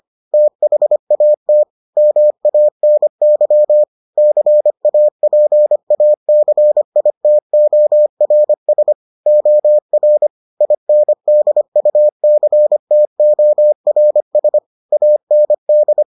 Morse code, heard as a keyed tone, about 25 wpm, 600 Hz. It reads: THAT MANY CAPACITORS OR INDUCTORS AND